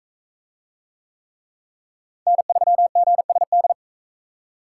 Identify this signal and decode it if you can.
Morse code, heard as a keyed tone, about 40 wpm, 700 Hz. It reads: N3GSD